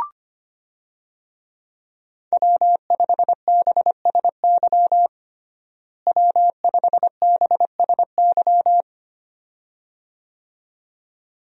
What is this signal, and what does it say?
Morse code, heard as a keyed tone, about 25 wpm, 700 Hz. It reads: W5BSY W5BSY
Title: USA Call Signs - 5 Characters - Repeated 2x - 25wpm